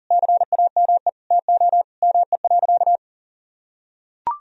Morse code, heard as a keyed tone, about 40 wpm, 700 Hz. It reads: CAME TO ME.
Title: Sentences from Top 1000 Words - Review Part 2 - 40wpm